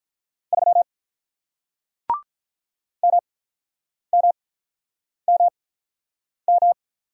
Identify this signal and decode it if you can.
Morse code, heard as a keyed tone, about 46 wpm, spaced as for 9 wpm, 700 Hz. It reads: 3 MMMM